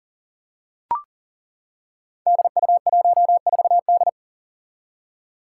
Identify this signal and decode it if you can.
Morse code, heard as a keyed tone, about 40 wpm, 700 Hz. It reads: DU14D
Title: Five Challenging Letter-Number Combinations 40wpm